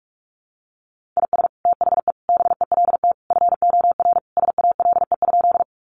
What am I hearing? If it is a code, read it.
Morse code, heard as a keyed tone, about 44 wpm, 700 Hz. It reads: IS THE BELT FOR SALE?
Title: Sentences from Top 1400 Words 45wpm